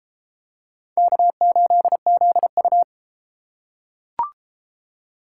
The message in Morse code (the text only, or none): K8ZU